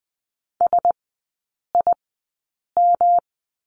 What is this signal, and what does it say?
Morse code, heard as a keyed tone, about 20 wpm, spaced as for 8 wpm, 700 Hz. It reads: SIM
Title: 107 Lesson 18e - Letter m - Three Characters 20wpm